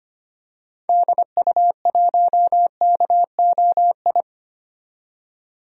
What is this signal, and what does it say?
Morse code, heard as a keyed tone, about 25 wpm, 700 Hz. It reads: DU1KOI
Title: DXCC-100 Call Signs - 6 Characters - 25wpm